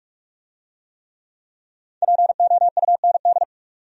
Morse code, heard as a keyed tone, about 44 wpm, 700 Hz. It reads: POUND